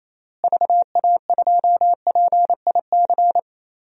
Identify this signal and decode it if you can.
Morse code, heard as a keyed tone, about 28 wpm, 700 Hz. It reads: VA2PIC
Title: DXCC-100 Call Signs - 6 Characters - Repeated 3x - 28wpm